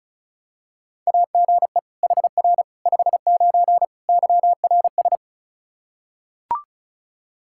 Morse code, heard as a keyed tone, about 36 wpm, 700 Hz. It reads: AGE HR 59 YRS
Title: Practice QSO - Exchange 3 - More Info - 35wpm